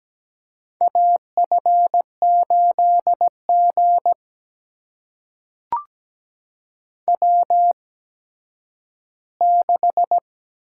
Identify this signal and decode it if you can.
Morse code, heard as a keyed tone, about 17 wpm, spaced as for 10 wpm, 700 Hz. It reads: AF8G W6
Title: USA Call Signs - 4 Characters - 8x Character Spacing - 17wpm